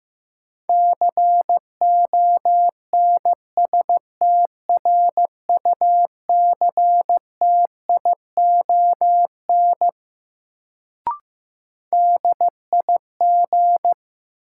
Morse code, heard as a keyed tone, about 15 wpm, 700 Hz. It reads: CONSTRUCTION DIG
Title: Sentences from Top 1600 Words - New Word Drill - 15wpm